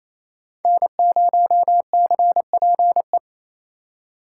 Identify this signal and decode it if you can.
Morse code, heard as a keyed tone, about 36 wpm, 700 Hz. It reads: N0CPE